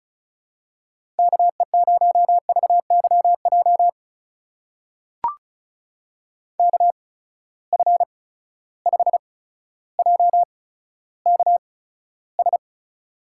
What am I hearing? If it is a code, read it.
Morse code, heard as a keyed tone, about 36 wpm, spaced as for 15 wpm, 700 Hz. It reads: KE0VYJ KF5JKS
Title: USA Call Signs - 6 Characters - 8x Character Spacing - 35wpm